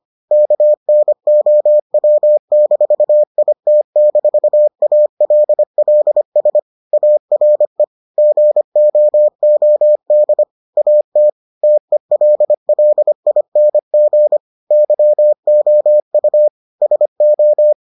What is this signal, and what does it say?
Morse code, heard as a keyed tone, about 25 wpm, 600 Hz. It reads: KNOW-IT-ALLS ARE GOOD AT TELLING YOU SO